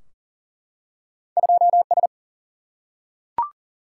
Morse code, heard as a keyed tone, about 38 wpm, 700 Hz. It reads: 2S
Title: Single-Letter Single-Number 40wpm